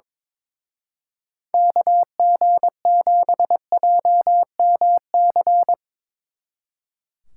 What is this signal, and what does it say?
Morse code, heard as a keyed tone, about 22 wpm, 700 Hz. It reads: KG7JMC
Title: Speed Racing - USA 2x3 Call Signs 22wpm